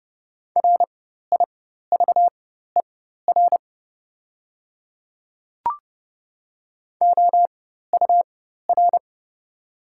Morse code, heard as a keyed tone, about 30 wpm, spaced as for 14 wpm, 700 Hz. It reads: RIVER OUR
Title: 4x Character Spacing - Top 300 Words - 30wpm